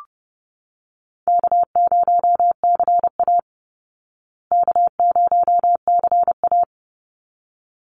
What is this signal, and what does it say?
Morse code, heard as a keyed tone, about 28 wpm, 700 Hz. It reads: K0CA K0CA
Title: Easy Call Signs Repeated 2x - 30wpm